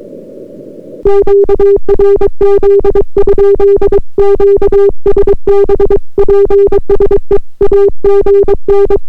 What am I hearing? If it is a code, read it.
Morse code, heard as a keyed tone, about 22 wpm, 400 Hz. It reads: QRZ? QSB PSE AGN